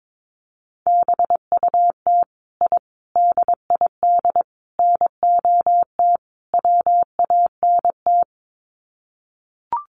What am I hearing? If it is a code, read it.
Morse code, heard as a keyed tone, about 22 wpm, 700 Hz. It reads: BUT I DID NOT WANT
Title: Sets of 5 Words Encore 22wpm